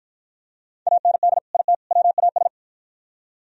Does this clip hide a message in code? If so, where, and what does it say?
Yes: Morse code from 0.9 to 2.5 s, heard as a keyed tone, about 52 wpm, 700 Hz. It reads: AND IT WAS